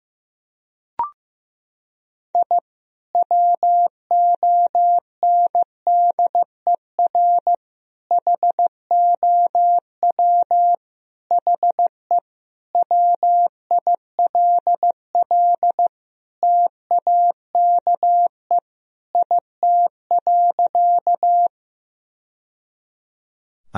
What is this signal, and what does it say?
Morse code, heard as a keyed tone, about 15 wpm, 700 Hz. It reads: I WONDER HOW HE WILL TAKE IT.
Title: Sentences from Top 1000 Words - Review Part 3 - 15wpm